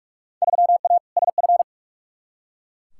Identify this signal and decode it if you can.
Morse code, heard as a keyed tone, about 46 wpm, 700 Hz. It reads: 3A SF
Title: Field Day Exchanges Encore - 45wpm